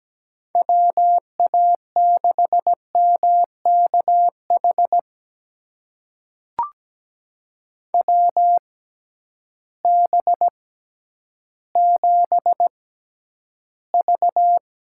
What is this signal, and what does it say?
Morse code, heard as a keyed tone, about 17 wpm, spaced as for 10 wpm, 700 Hz. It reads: WA6MKH WB7V
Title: USA Call Signs - 6 Characters - 6x Character Spacing - 17wpm